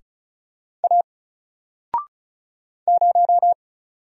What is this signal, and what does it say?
Morse code, heard as a keyed tone, about 36 wpm, 700 Hz. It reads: A 0
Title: Single Letter-Number Alt 35wpm